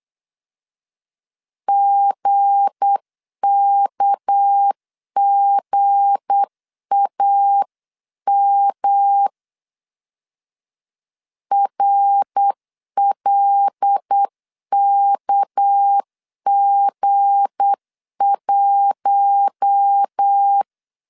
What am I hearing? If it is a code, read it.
Morse code, heard as a keyed tone, about 8 wpm, 800 Hz. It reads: GKGAM RLKG1
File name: LEZIONI / 7 GF9Z / 7 G 8wpm 3118